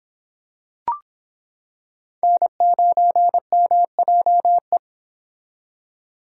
Morse code, heard as a keyed tone, about 26 wpm, 700 Hz. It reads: N9MJE